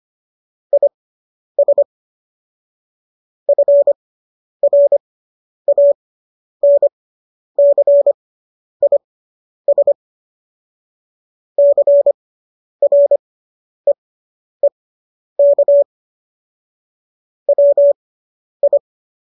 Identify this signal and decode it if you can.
Morse code, heard as a keyed tone, about 25 wpm, spaced as for 10 wpm, 570 Hz. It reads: IS FRANCIS CREEK WI